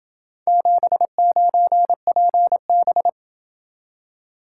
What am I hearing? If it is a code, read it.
Morse code, heard as a keyed tone, about 27 wpm, 700 Hz. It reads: 79PB